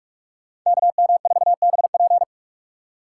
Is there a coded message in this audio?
Yes: Morse code from 0.7 to 2.2 s, heard as a keyed tone, about 44 wpm, 700 Hz. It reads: KM4BP